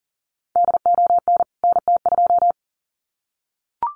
Morse code, heard as a keyed tone, about 39 wpm, 700 Hz. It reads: DON NT2
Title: CWT Exchange 40wpm